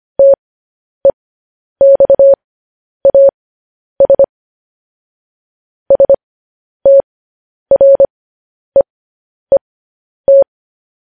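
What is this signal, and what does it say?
Morse code, heard as a keyed tone, about 25 wpm, spaced as for 10 wpm, 550 Hz. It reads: TEXAS STREET
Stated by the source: Morse code practice of news headlines at 10 wpm.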